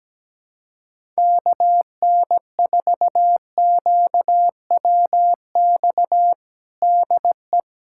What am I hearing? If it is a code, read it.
Morse code, heard as a keyed tone, about 17 wpm, 700 Hz. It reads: KN4QWX DE
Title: USA Callsign DE Callsign - 6 Characters - 17wpm